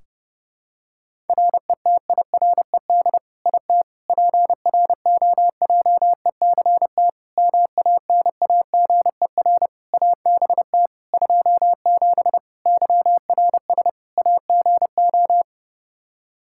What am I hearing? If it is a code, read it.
Morse code, heard as a keyed tone, about 30 wpm, 700 Hz. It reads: RETIRED IT PROJECT MANAGER ABT 27 YRS AGO
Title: Practice QSO - Exchange 3 - More Info - 30wpm